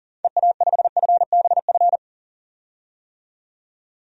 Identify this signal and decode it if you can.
Morse code, heard as a keyed tone, about 40 wpm, 700 Hz. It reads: EA5FBF
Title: DXCC-100 Call Signs - 6 Characters - 40wpm